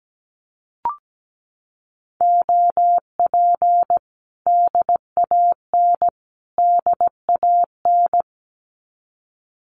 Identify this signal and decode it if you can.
Morse code, heard as a keyed tone, about 17 wpm, 700 Hz. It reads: OP DAN DAN
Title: Common QSO Phrases Encore 17wpm